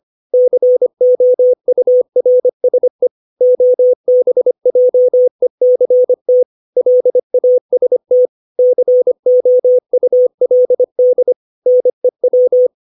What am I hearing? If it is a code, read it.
Morse code, heard as a keyed tone, about 25 wpm, 500 Hz. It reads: COURSE OBJECT LAST COULD NEW
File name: 632 words / 632 Words 25WPM